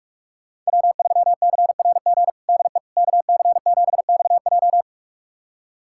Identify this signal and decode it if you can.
Morse code, heard as a keyed tone, about 46 wpm, 700 Hz. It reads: W3CRG DE KC7XJ
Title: USA Callsign DE Callsign - 5 Characters - 45wpm